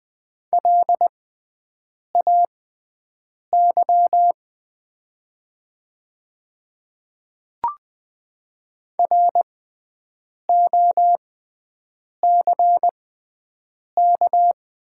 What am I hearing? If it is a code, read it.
Morse code, heard as a keyed tone, about 20 wpm, spaced as for 7 wpm, 700 Hz. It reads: LAY ROCK
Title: 6x Character Spacing - Top 500 Words - 20wpm